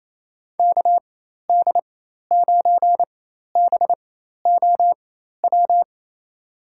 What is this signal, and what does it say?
Morse code, heard as a keyed tone, about 28 wpm, spaced as for 13 wpm, 700 Hz. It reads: KD9BOW
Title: USA Call Signs - 6 Characters - 4x Character Spacing - 28wpm